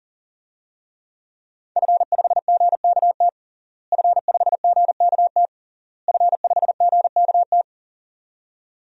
Morse code, heard as a keyed tone, about 37 wpm, 700 Hz. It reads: F5GKT F5GKT F5GKT
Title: DXCC-100 Call Signs - 5 Characters - Repeated 3x - 40wpm